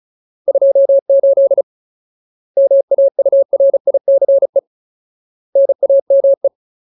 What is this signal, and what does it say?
Morse code, heard as a keyed tone, about 34 wpm, 550 Hz. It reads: 28 MAURICE NAME
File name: Ultimate Word List / 35WPM Ultimate